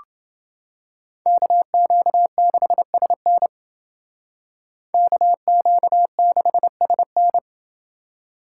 Morse code, heard as a keyed tone, about 29 wpm, 700 Hz. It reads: KQ6SN KQ6SN